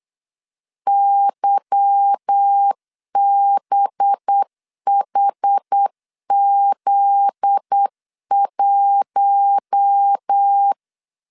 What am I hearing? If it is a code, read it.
Morse code, heard as a keyed tone, about 8 wpm, 800 Hz. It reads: YBHZ1
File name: LEZIONI / 8 YJ45 / 8 YJ4 8wpm 3246